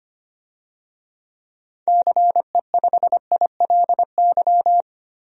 Morse code, heard as a keyed tone, about 25 wpm, 700 Hz. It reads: CE5ILY
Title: DXCC-100 Call Signs - 6 Characters - 25wpm